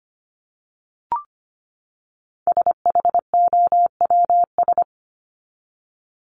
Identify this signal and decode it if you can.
Morse code, heard as a keyed tone, about 25 wpm, 700 Hz. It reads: SHOWS
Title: ES:S Suffix - Top 100 Words with Base Words 25wpm